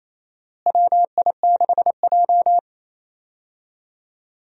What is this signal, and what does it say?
Morse code, heard as a keyed tone, about 27 wpm, 700 Hz. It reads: WI6J
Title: USA Call Signs - 4 Characters - Repeated 2x - 28wpm